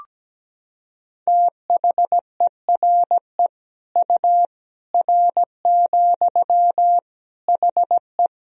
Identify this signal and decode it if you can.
Morse code, heard as a keyed tone, about 17 wpm, 700 Hz. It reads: THERE U R, HE
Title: Sentences from Top 100 Words and Abbreviations 17wpm